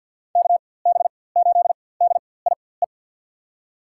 Morse code, heard as a keyed tone, about 46 wpm, spaced as for 23 wpm, 700 Hz. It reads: KB8DIE
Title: USA Call Signs - 4x Character Spacing - 50wpm